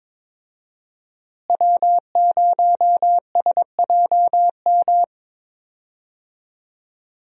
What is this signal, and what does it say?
Morse code, heard as a keyed tone, about 23 wpm, 700 Hz. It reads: W0SJM
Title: USA Call Signs - 5 Characters - Repeated 3x - 22wpm